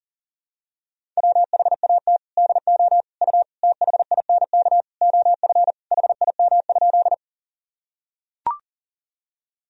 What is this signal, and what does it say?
Morse code, heard as a keyed tone, about 40 wpm, 700 Hz. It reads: WHAT DO U THINK OF HIM?
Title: Sentences from Top 200 Words and Abbreviations 40wpm